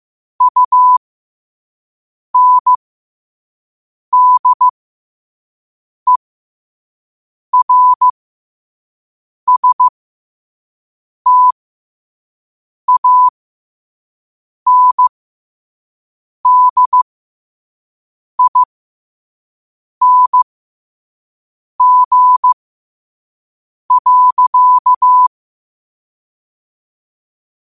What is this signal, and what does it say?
Morse code, heard as a keyed tone, about 15 wpm, spaced as for 5 wpm, 1000 Hz. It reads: UNDERSTANDING.
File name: qotdfiles / 5wpm